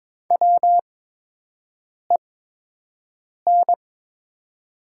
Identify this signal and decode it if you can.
Morse code, heard as a keyed tone, about 23 wpm, spaced as for 6 wpm, 700 Hz. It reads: WEN
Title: 8x Character Spacing - Top 200 Words - 22wpm